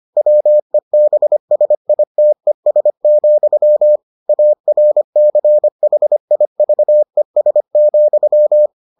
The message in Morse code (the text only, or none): WEBSITES, ARCHIVES,